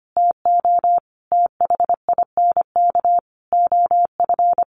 Morse code, heard as a keyed tone, about 25 wpm, 700 Hz. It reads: TO THINK OF
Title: Sentences from Top 800 Words 25wpm